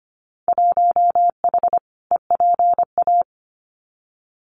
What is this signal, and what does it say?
Morse code, heard as a keyed tone, about 25 wpm, 700 Hz. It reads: 1H EPA
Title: Winter Field Day Exchanges 25wpm